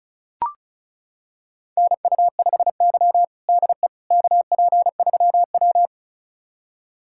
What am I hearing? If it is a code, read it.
Morse code, heard as a keyed tone, about 36 wpm, 700 Hz. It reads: NU5Y DE KP3W
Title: USA Callsign DE Callsign - 4 Characters - 35wpm